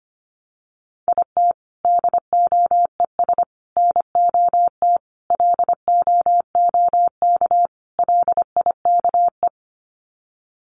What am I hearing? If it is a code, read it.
Morse code, heard as a keyed tone, about 25 wpm, 700 Hz. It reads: IT DOES NOT LOOK LIKE
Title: Sets of 5 Words Encore2 25wpm